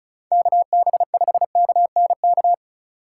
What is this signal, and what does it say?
Morse code, heard as a keyed tone, about 36 wpm, 690 Hz. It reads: KB5KNK
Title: USA Call Signs - 6 Characters - 4x Character Spacing - 35wpm